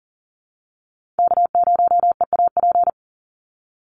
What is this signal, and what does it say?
Morse code, heard as a keyed tone, about 40 wpm, 700 Hz. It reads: K0EAP
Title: NA SOTA Chaser Callsigns 40wpm